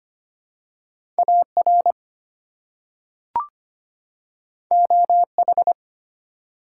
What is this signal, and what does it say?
Morse code, heard as a keyed tone, about 25 wpm, 700 Hz. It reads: AR OH
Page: morse-code-ninja-practice-25wpm